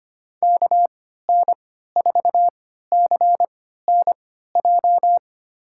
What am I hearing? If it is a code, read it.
Morse code, heard as a keyed tone, about 25 wpm, spaced as for 14 wpm, 700 Hz. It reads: KN4CNJ
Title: USA Call Signs - 6 Characters - 3x Character Spacing - 25wpm